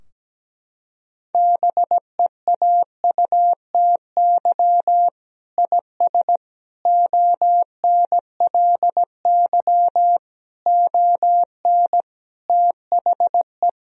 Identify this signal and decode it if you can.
Morse code, heard as a keyed tone, about 17 wpm, 700 Hz. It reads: BEAUTY IS ONLY ON THE
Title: Sentences from Top 1000 Words - Review Part 3 - 17wpm